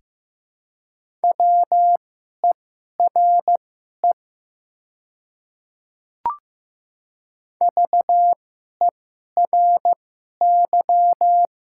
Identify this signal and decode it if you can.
Morse code, heard as a keyed tone, about 15 wpm, spaced as for 11 wpm, 700 Hz. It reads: WERE VERY
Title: 2x Character Spacing - Top 300 Words - 15wpm